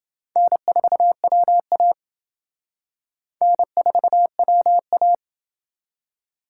Morse code, heard as a keyed tone, about 29 wpm, 700 Hz. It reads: N4WA N4WA